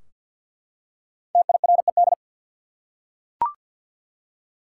Morse code, heard as a keyed tone, about 50 wpm, 700 Hz. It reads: TIRED
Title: Five Letter Words - 50wpm